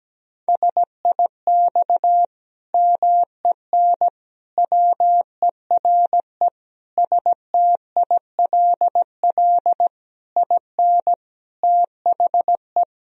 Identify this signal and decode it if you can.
Morse code, heard as a keyed tone, about 17 wpm, 700 Hz. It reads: SIX MEN WERE STILL IN THE
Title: Sentences from Top 1000 Words - Review Part 2 - 17wpm